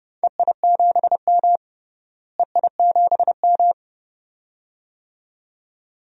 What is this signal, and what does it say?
Morse code, heard as a keyed tone, about 30 wpm, 700 Hz. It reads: EI7M EI7M
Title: DXCC-100 Call Signs - 4 Characters - Repeated 2x - 30wpm